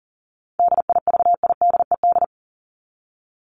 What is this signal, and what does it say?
Morse code, heard as a keyed tone, about 40 wpm, 700 Hz. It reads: DIVIDED
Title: ED Suffix - Top 100 Words with Base Words 40wpm